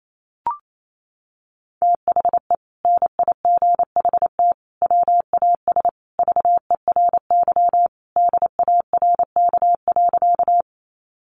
Morse code, heard as a keyed tone, about 28 wpm, 700 Hz. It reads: THE NIGHT WAS VERY DARK.
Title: Sentences from Top 500 Words 28wpm